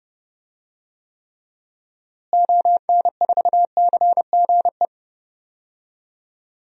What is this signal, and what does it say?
Morse code, heard as a keyed tone, about 30 wpm, 700 Hz. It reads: ON4CGE
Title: DXCC-100 Call Signs - 6 Characters - Repeated 2x - 30wpm